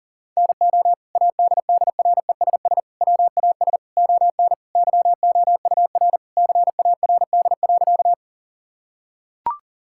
Morse code, heard as a keyed tone, about 38 wpm, 700 Hz. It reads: NO ADDRESS WAS ON YOUR CARD.
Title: Sentences from Top 1000 Words - Review Part 6 - 40wpm